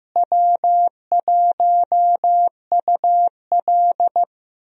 Morse code, heard as a keyed tone, about 16 wpm, 700 Hz. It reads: W1UL
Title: USA Call Signs - 4 Characters - Repeated 3x - 15wpm